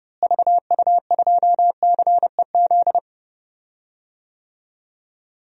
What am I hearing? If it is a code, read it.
Morse code, heard as a keyed tone, about 30 wpm, 700 Hz. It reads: VU2CEZ
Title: DXCC-100 Call Signs - 6 Characters - Repeated 2x - 30wpm